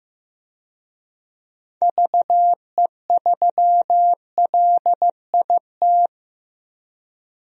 Morse code, heard as a keyed tone, about 15 wpm, 700 Hz. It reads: VE3LIT
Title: DXCC-100 Call Signs - 6 Characters - Repeated 3x - 15wpm